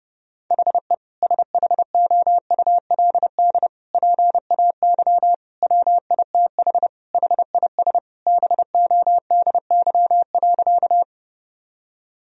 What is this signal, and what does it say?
Morse code, heard as a keyed tone, about 30 wpm, 700 Hz. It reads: HE SHOULD PAY WITH HIS BODY.
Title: Sentences from Top 1000 Words - Review Part 2 - 30wpm